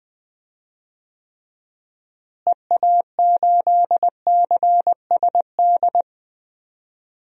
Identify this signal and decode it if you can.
Morse code, heard as a keyed tone, about 20 wpm, 700 Hz. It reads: EA8CSD
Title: DXCC-100 Call Signs - 6 Characters - Repeated 3x - 20wpm